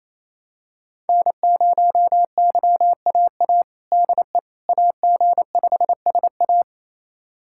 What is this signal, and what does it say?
Morse code, heard as a keyed tone, about 28 wpm, 700 Hz. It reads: N0YAA DE AG5SA
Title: USA Callsign DE Callsign 28wpm